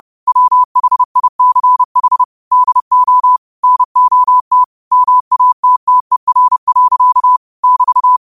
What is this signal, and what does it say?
Morse code, heard as a keyed tone, about 30 wpm, 1000 Hz. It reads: WHICH DO NOT MATTER. <BT>
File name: qotdfiles / 30wpm